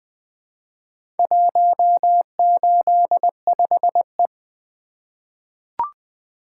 Morse code, heard as a keyed tone, about 20 wpm, 700 Hz. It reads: 185E
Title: Four Letter-Number Combinations - 20wpm